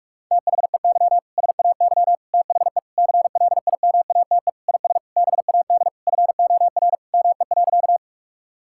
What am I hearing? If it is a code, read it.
Morse code, heard as a keyed tone, about 45 wpm, 700 Hz. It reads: THEY SAY THE CLIMATE IS BAD FOR ME.
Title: Sentences from Top 1600 Words 45wpm